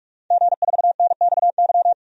Morse code, heard as a keyed tone, about 46 wpm, 700 Hz. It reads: G4NXY
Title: DXCC-100 Call Signs - 5 Characters - 45wpm